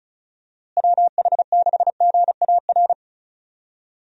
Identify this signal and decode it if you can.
Morse code, heard as a keyed tone, about 34 wpm, 700 Hz. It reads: WH6GAR
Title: USA Call Signs - 6 Characters - Repeated 3x - 35wpm